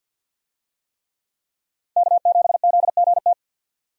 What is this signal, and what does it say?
Morse code, heard as a keyed tone, about 52 wpm, 700 Hz. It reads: K7ZGT
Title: USA Call Signs - 5 Characters - 50wpm